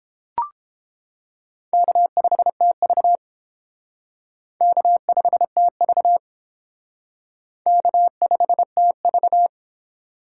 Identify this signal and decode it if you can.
Morse code, heard as a keyed tone, about 30 wpm, 700 Hz. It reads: K5TV K5TV K5TV